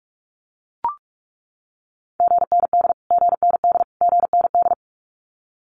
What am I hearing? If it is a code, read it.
Morse code, heard as a keyed tone, about 44 wpm, 700 Hz. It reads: GND GND GND
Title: Abbreviations Repeated 3x - 45wpm